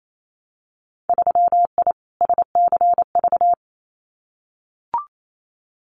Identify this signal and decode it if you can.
Morse code, heard as a keyed tone, about 28 wpm, 700 Hz. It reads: 3I SCV
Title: Winter Field Day Exchanges Encore 28wpm